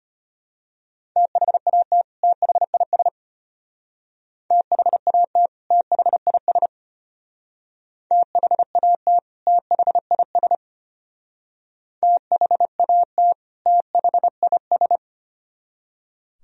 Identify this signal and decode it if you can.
Morse code, heard as a keyed tone, about 33 wpm, 700 Hz. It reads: THAT THIS THAT THIS THAT THIS THAT THIS